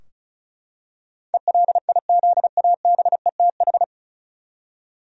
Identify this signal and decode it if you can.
Morse code, heard as a keyed tone, about 36 wpm, 700 Hz. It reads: ELIZABETH
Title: USA Common Names - 35wpm